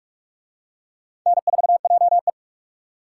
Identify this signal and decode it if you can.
Morse code, heard as a keyed tone, about 39 wpm, 700 Hz. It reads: N4JE